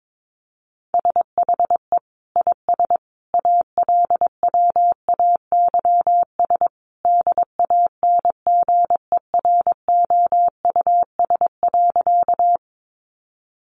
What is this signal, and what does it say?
Morse code, heard as a keyed tone, about 22 wpm, 700 Hz. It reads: SHE IS ALWAYS DANGEROUS.
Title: Sentences from Top 1200 Words 22wpm